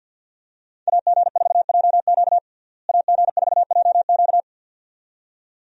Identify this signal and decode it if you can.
Morse code, heard as a keyed tone, about 50 wpm, 700 Hz. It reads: AG4JQ AG4JQ
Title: USA Call Signs - 5 Characters - Repeated 2x - 50wpm